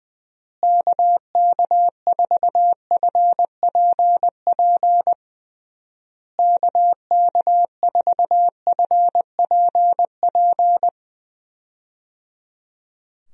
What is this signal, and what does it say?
Morse code, heard as a keyed tone, about 20 wpm, 700 Hz. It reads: KK4FPP KK4FPP
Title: USA Call Signs - 6 Characters - Repeated 3x - 20wpm